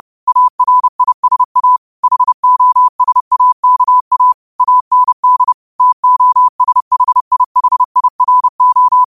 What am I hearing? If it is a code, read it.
Morse code, heard as a keyed tone, about 30 wpm, 1000 Hz. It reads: ARISA HOSAKA AND TOSHIHIRO
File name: qotdfiles / 30wpm